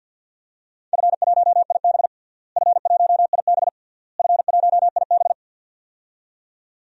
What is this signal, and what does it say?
Morse code, heard as a keyed tone, about 52 wpm, 700 Hz. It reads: F1IB F1IB F1IB
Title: DXCC-100 Call Signs - 4 Characters - Repeated 3x - 50wpm